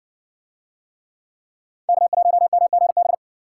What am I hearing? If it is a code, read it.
Morse code, heard as a keyed tone, about 56 wpm, 700 Hz. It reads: K1MGB